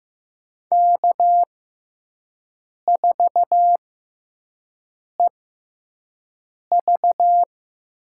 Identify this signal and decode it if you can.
Morse code, heard as a keyed tone, about 15 wpm, spaced as for 5 wpm, 700 Hz. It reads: K4EV
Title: USA Call Signs - 4 Characters - 6x Character Spacing - 15wpm